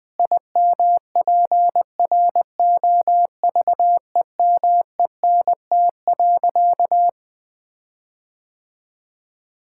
Morse code, heard as a keyed tone, about 21 wpm, 700 Hz. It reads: IMPROVEMENT.
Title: Sentences with 8-Word Phrases - 20wpm